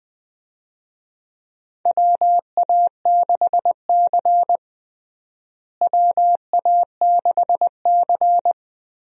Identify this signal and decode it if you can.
Morse code, heard as a keyed tone, about 21 wpm, 700 Hz. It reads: WA6C WA6C
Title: USA Call Signs - 4 Characters - Repeated 3x - 20wpm